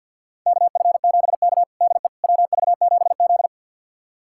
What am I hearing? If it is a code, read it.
Morse code, heard as a keyed tone, about 46 wpm, 700 Hz. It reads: KF7K DE WV7Z
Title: USA Callsign DE Callsign - 4 Characters - 50wpm